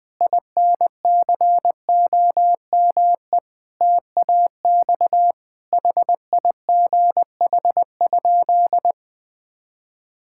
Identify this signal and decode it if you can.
Morse code, heard as a keyed tone, about 20 wpm, 700 Hz. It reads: INCOME TAX HIGH?
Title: Sentences from Top 1000 Words - Review Part 5 - 20wpm